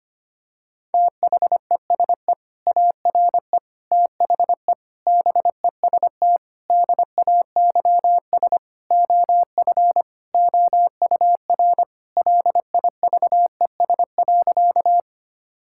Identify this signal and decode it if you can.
Morse code, heard as a keyed tone, about 25 wpm, 700 Hz. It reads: THESE ARE THE BEST DAYS OF OUR LIVES.
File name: practice / 25wpm / Sentences with 2-Word Phrases - 25wpm